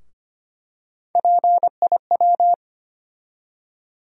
Morse code, heard as a keyed tone, about 25 wpm, 700 Hz. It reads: PIW
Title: USA Call Sign Postfixes - 3 Letters 25wpm